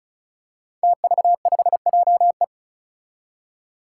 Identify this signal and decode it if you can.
Morse code, heard as a keyed tone, about 34 wpm, 700 Hz. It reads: TV5JE
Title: Five Challenging Letter-Number Combinations 35wpm